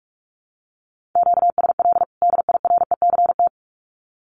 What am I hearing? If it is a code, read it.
Morse code, heard as a keyed tone, about 44 wpm, 700 Hz. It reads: QSL DIRECT
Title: Common QSO Phrases 45wpm